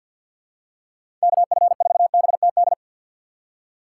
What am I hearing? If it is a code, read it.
Morse code, heard as a keyed tone, about 49 wpm, 700 Hz. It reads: KF4BTD
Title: USA Call Signs - 6 Characters - 3x Character Spacing - 50wpm